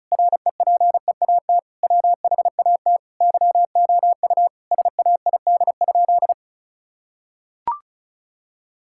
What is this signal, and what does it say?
Morse code, heard as a keyed tone, about 36 wpm, 700 Hz. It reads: REPEAT WHAT YOU SAID?
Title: Sentences from Top 1600 Words 35wpm